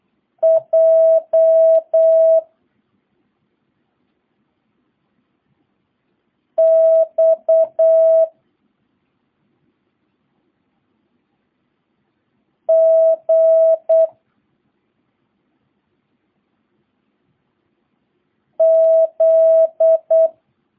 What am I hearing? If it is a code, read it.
Morse code, heard as a keyed tone, about 8 wpm, 650 Hz. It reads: J X G Z